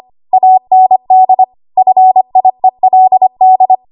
Morse code, heard as a keyed tone, about 25 wpm, 750 Hz. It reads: AND FIELD
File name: Archive / 25 WPM / 25WPM